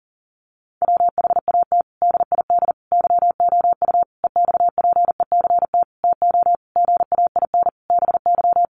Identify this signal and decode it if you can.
Morse code, heard as a keyed tone, about 40 wpm, 700 Hz. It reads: WHAT DID YOU EXPECT TO GAIN BY
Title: Sentences from Top 1100 Words 40wpm